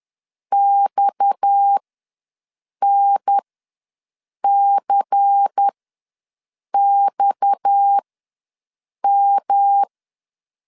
Morse code, heard as a keyed tone, about 10 wpm, spaced as for 6 wpm, 800 Hz. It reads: XNCXM
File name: LEZIONI / 6 PKX0 / 6 PKX ws 10wpm 3706